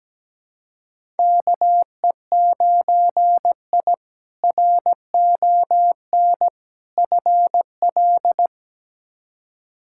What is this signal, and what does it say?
Morse code, heard as a keyed tone, about 17 wpm, 700 Hz. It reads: KE9I RON FL
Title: K1USN SST QSO Practice - 17wpm